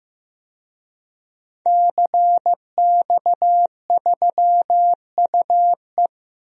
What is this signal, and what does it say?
Morse code, heard as a keyed tone, about 15 wpm, 700 Hz. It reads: CX3UE
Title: DXCC-100 Call Signs - 5 Characters - Repeated 3x - 15wpm